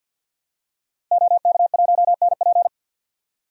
Morse code, heard as a keyed tone, about 52 wpm, 700 Hz. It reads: OK1NP
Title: DXCC-100 Call Signs - 5 Characters - Repeated 3x - 50wpm